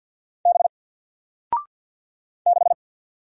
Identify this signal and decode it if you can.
Morse code, heard as a keyed tone, about 48 wpm, 700 Hz. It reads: B 6
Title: Single Letter-Number Alt 50wpm